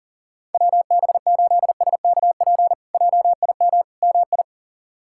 Morse code, heard as a keyed tone, about 40 wpm, 700 Hz. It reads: WB8SKP JIM MI
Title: K1USN SST QSO Practice - 40wpm